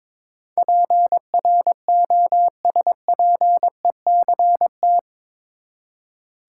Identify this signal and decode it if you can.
Morse code, heard as a keyed tone, about 22 wpm, 700 Hz. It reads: PROSPECT
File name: practice / 22wpm / Eight Letter Words - 22wpm